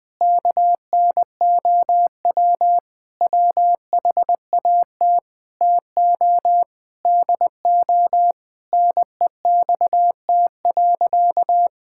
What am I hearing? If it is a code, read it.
Morse code, heard as a keyed tone, about 20 wpm, 700 Hz. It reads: KNOW WHAT TO DO NEXT.
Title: Sentences with 5-Word Phrases - Spoken Beginning - 20wpm